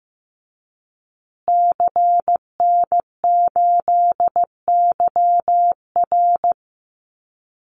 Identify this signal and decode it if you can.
Morse code, heard as a keyed tone, about 15 wpm, 700 Hz. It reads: CN8YR
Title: Call Signs Repeated 2x - 15wpm